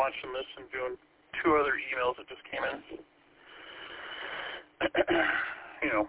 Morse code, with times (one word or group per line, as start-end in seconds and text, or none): none